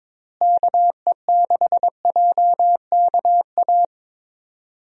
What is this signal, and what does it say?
Morse code, heard as a keyed tone, about 22 wpm, 700 Hz. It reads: KE6JKA
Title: USA Call Signs - 6 Characters - Repeated 2x - 22wpm